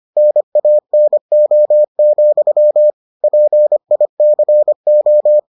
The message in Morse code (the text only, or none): NANO, PICO